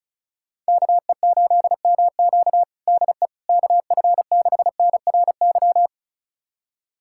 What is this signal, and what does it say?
Morse code, heard as a keyed tone, about 34 wpm, 700 Hz. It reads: KE8MQ DE KF6NRY
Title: USA Callsign DE Callsign 35wpm